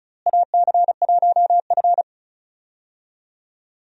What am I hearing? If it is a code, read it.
Morse code, heard as a keyed tone, about 36 wpm, 700 Hz. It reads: AC1F
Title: USA Call Signs - 4 Characters - Repeated 2x - 35wpm